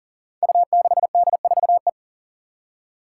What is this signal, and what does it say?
Morse code, heard as a keyed tone, about 44 wpm, 700 Hz. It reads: U6D4E
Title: Five Challenging Letter-Number Combinations 40wpm